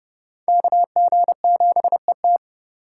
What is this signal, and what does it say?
Morse code, heard as a keyed tone, about 26 wpm, 700 Hz. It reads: KG7ET